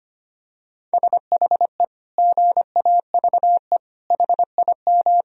Morse code, heard as a keyed tone, about 25 wpm, 700 Hz. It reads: SHE GAVE HIM
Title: Sentences from Top 1000 Words - Review Part 1 - 25wpm